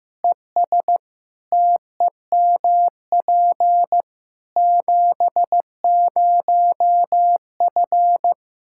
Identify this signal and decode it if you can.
Morse code, heard as a keyed tone, about 15 wpm, 700 Hz. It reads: ES TEMP 70F
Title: Practice QSO - Exchange 2 - Info - 15wpm